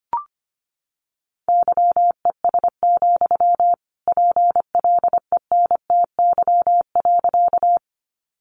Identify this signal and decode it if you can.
Morse code, heard as a keyed tone, about 25 wpm, 700 Hz. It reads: YES, PLENTY.
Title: Sentences from Top 1300 Words 25wpm